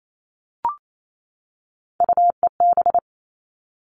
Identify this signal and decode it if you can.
Morse code, heard as a keyed tone, about 28 wpm, 700 Hz. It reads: UEB
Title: All Cut Numbers - 3-Digit 28wpm